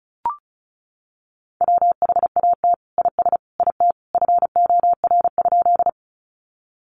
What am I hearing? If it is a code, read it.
Morse code, heard as a keyed tone, about 34 wpm, 700 Hz. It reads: WHAT IS IT FOR?
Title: Sentences from Top 100 Words 35wpm